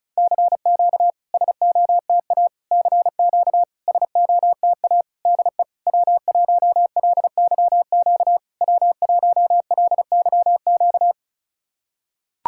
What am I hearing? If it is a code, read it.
Morse code, heard as a keyed tone, about 36 wpm, 700 Hz. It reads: CQ SOTA CQ SOTA DE W1LYQ W1LYQ
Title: SOTA QSO Practice 35wpm